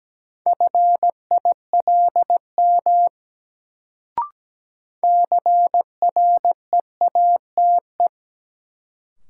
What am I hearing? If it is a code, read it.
Morse code, heard as a keyed tone, about 17 wpm, 700 Hz. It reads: FILM CREATE
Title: Sentences from Top 1000 Words - Review Part 3 - New Word Drill - 17wpm